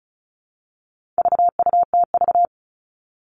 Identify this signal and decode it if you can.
Morse code, heard as a keyed tone, about 34 wpm, 700 Hz. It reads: VUTV
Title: All Cut Numbers - 4-Digit 35wpm